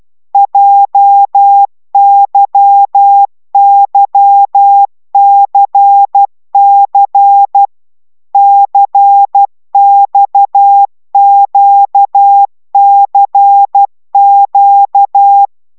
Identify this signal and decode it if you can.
Morse code, heard as a keyed tone, about 12 wpm, 800 Hz. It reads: JYYCC CXQCQ